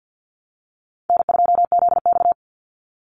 Morse code, heard as a keyed tone, about 50 wpm, 700 Hz. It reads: N2ZX
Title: Call Signs Repeated 2x - 50wpm